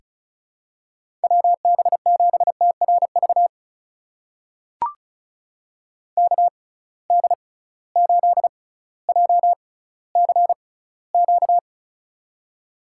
Morse code, heard as a keyed tone, about 36 wpm, spaced as for 18 wpm, 700 Hz. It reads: WB7TRV KD8JCQ
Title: USA Call Signs - 6 Characters - 6x Character Spacing - 35wpm